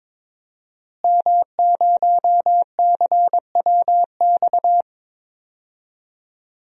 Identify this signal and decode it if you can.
Morse code, heard as a keyed tone, about 22 wpm, 700 Hz. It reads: M0CWX
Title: DXCC-100 Call Signs - 5 Characters - Repeated 3x - 22wpm